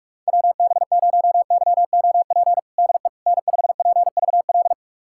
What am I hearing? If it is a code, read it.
Morse code, heard as a keyed tone, about 44 wpm, 700 Hz. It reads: WB0YOP DE N5PVL
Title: USA Callsign DE Callsign 45wpm